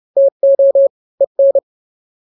Morse code, heard as a keyed tone, about 30 wpm, 550 Hz. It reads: TO EN
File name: n-grams / 2-grams 30 25 WPM